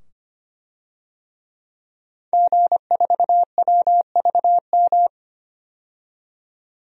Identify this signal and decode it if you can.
Morse code, heard as a keyed tone, about 25 wpm, 700 Hz. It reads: G4WVM
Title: DXCC-100 Call Signs - 5 Characters - 25wpm